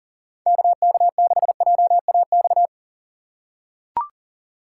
Morse code, heard as a keyed tone, about 40 wpm, 700 Hz. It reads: KK6JAX
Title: USA Call Signs - 6 Characters - 6x Character Spacing - 40wpm